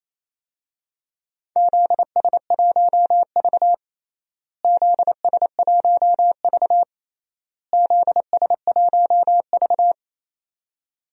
Morse code, heard as a keyed tone, about 28 wpm, 700 Hz. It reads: ZS1V ZS1V ZS1V
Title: DXCC-100 Call Signs - 4 Characters - Repeated 3x - 28wpm